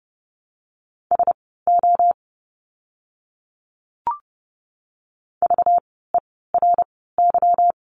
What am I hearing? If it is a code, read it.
Morse code, heard as a keyed tone, about 30 wpm, spaced as for 17 wpm, 700 Hz. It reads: SO VERY
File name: course / 30:10wpm / 230 Lesson 37f - Letter x - Words 30wpm